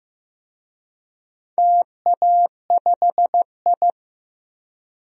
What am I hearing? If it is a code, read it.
Morse code, heard as a keyed tone, about 15 wpm, 700 Hz. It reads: TA5I
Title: DXCC-100 Call Signs - 4 Characters - Repeated 3x - 15wpm